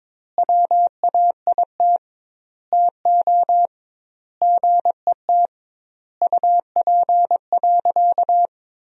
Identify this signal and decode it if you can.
Morse code, heard as a keyed tone, about 22 wpm, 700 Hz. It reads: WAIT TO GET UP.
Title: 2x Word-Spacing - Sentences from Top 500 Words - 22wpm